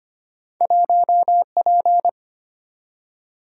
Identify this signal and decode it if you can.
Morse code, heard as a keyed tone, about 25 wpm, 700 Hz. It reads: 1P
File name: Two Letter-Number Combinations - 25wpm